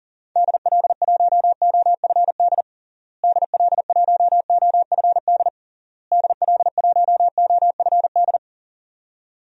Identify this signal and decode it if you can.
Morse code, heard as a keyed tone, about 40 wpm, 700 Hz. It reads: DL1OFD DL1OFD DL1OFD
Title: DXCC-100 Call Signs - 6 Characters - Repeated 3x - 40wpm